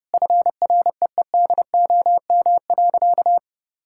Morse code, heard as a keyed tone, about 32 wpm, 700 Hz. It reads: FREEDOM.
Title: Sentences from Top 1000 Words - Review Part 5 - 30wpm